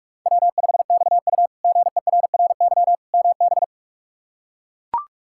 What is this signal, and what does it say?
Morse code, heard as a keyed tone, about 41 wpm, 700 Hz. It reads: W5XU GERRY MB
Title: K1USN SST QSO Practice - 45wpm